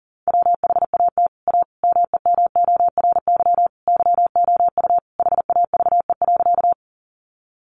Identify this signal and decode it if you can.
Morse code, heard as a keyed tone, about 40 wpm, 700 Hz. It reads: WHAT A MEMORY YOU HAVE.
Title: Sentences from Top 1300 Words 40wpm